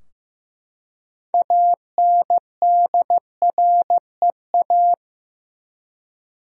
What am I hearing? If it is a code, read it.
Morse code, heard as a keyed tone, about 15 wpm, 700 Hz. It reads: ANDREA
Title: USA Common Names - 15wpm